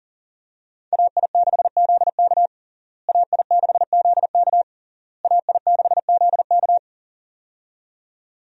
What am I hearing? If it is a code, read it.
Morse code, heard as a keyed tone, about 40 wpm, 700 Hz. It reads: AI6ZK AI6ZK AI6ZK
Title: USA Call Signs - 5 Characters - Repeated 3x - 40wpm